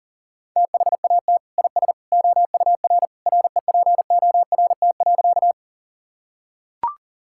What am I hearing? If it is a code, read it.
Morse code, heard as a keyed tone, about 37 wpm, 700 Hz. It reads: THAT IS OUR REPORT.
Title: Sentences from Top 1000 Words - Review Part 6 - 40wpm